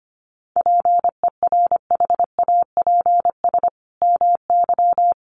Morse code, heard as a keyed tone, about 25 wpm, 700 Hz. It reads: PERHAPS MY
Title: Sentences from Top 1100 Words 25wpm